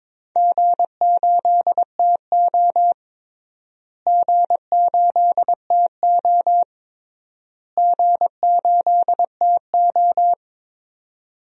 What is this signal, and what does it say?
Morse code, heard as a keyed tone, about 22 wpm, 700 Hz. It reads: G8TO G8TO G8TO
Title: DXCC-100 Call Signs - 4 Characters - Repeated 3x - 22wpm